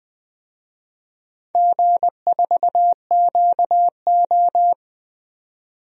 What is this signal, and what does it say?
Morse code, heard as a keyed tone, about 20 wpm, 700 Hz. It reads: G4QO
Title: DXCC-100 Call Signs - 4 Characters - 20wpm